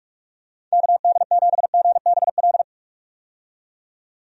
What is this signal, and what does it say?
Morse code, heard as a keyed tone, about 46 wpm, 700 Hz. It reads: KD7GBL
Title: USA Call Signs - 6 Characters - 45wpm